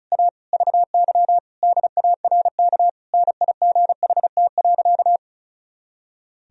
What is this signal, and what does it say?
Morse code, heard as a keyed tone, about 34 wpm, 700 Hz. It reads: A VY DARK NIGHT.
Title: Sentences from Top 500 Words and Abbreviations 35wpm